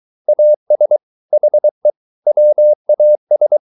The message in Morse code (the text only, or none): AS HE WAS